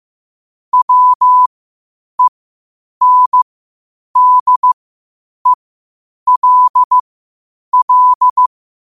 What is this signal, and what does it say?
Morse code, heard as a keyed tone, about 15 wpm, spaced as for 8 wpm, 1000 Hz. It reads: WENDELL
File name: qotdfiles / 8wpm